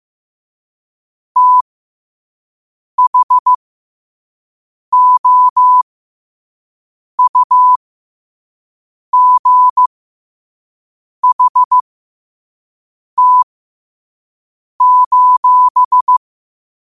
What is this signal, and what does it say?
Morse code, heard as a keyed tone, about 15 wpm, spaced as for 5 wpm, 1000 Hz. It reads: THOUGHT:
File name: qotdfiles / 5wpm